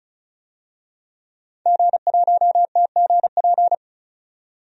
Morse code, heard as a keyed tone, about 36 wpm, 700 Hz. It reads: G1TGP
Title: DXCC-100 Call Signs - 5 Characters - 35wpm